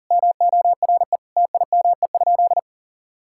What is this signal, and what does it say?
Morse code, heard as a keyed tone, about 40 wpm, 700 Hz. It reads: MORE TIME?
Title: Sentences from Top 1000 Words - Review Part 5 - 40wpm